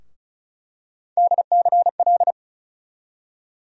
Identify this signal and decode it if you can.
Morse code, heard as a keyed tone, about 36 wpm, 700 Hz. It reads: DCL
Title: USA Call Sign Postfixes - 3 Letters 35wpm